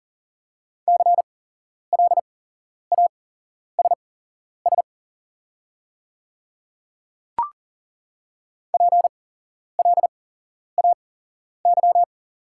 Morse code, heard as a keyed tone, about 41 wpm, spaced as for 11 wpm, 700 Hz. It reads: CLASS PLAY
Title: 8x Character Spacing - Top 500 Words - 40wpm